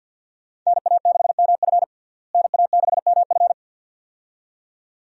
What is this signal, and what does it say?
Morse code, heard as a keyed tone, about 49 wpm, 700 Hz. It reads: NA6MF NA6MF
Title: USA Call Signs - 5 Characters - Repeated 2x - 50wpm